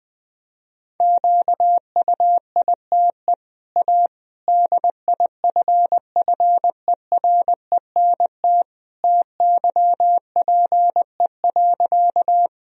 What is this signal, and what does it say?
Morse code, heard as a keyed tone, about 21 wpm, 700 Hz. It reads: QUITE A DIFFERENT TYPE.
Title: Sentences from Top 1000 Words - Review Part 3 - 20wpm